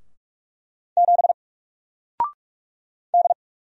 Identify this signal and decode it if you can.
Morse code, heard as a keyed tone, about 44 wpm, 700 Hz. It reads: 7 D
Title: Single Letter-Number Alt 45wpm